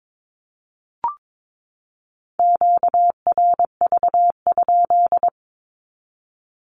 Morse code, heard as a keyed tone, about 22 wpm, 700 Hz. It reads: QRV?
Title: Q-Codes 22wpm